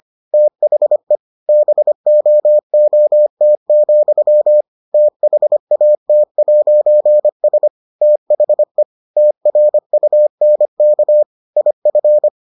Morse code, heard as a keyed tone, about 25 wpm, 600 Hz. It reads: THE BOOT, THAT'S THE TRUNK IF